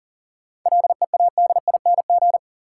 Morse code, heard as a keyed tone, about 40 wpm, 700 Hz. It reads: LEADING
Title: Sentences from Top 1100 Words - New Word Drill Alt - 40wpm